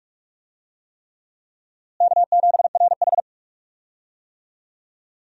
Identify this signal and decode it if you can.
Morse code, heard as a keyed tone, about 41 wpm, 700 Hz. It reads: K7RH
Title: USA Call Signs - 4 Characters - Repeated 3x - 45wpm